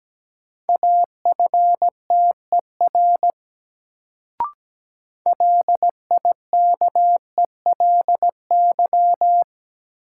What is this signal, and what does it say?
Morse code, heard as a keyed tone, about 17 wpm, 700 Hz. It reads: AFTER LIKELY
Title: Sets of 2 Words - New Word Drill - 17wpm